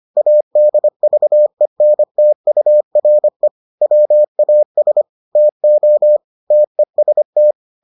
Morse code, heard as a keyed tone, about 25 wpm, 600 Hz. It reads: ADVENTURE WAS TO TEST